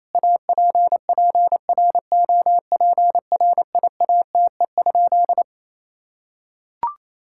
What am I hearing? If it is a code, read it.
Morse code, heard as a keyed tone, about 28 wpm, 700 Hz. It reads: APPROPRIATE?
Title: Sentences from Top 1000 Words - Review Part 5 - 28wpm